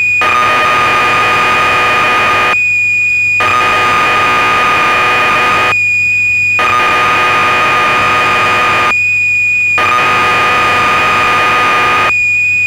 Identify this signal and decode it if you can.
Cassette data (Kansas City type).
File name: Acorn User/1984-12 Electron Programs